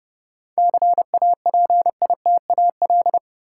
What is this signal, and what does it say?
Morse code, heard as a keyed tone, about 32 wpm, 700 Hz. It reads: CAPITAL
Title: Sentences from Top 1000 Words - New Word Drill - 30wpm